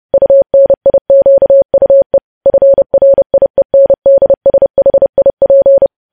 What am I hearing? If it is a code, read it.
Morse code, heard as a keyed tone, about 30 wpm, 550 Hz. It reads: UNIQUE FRIENDSHIP
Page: news headlines in morse code at 30 wpm